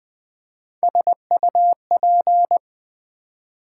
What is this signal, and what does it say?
Morse code, heard as a keyed tone, about 20 wpm, 700 Hz. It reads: SUP
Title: Build A Word - 3 Letters At A Time - 8-Letter Words - 20wpm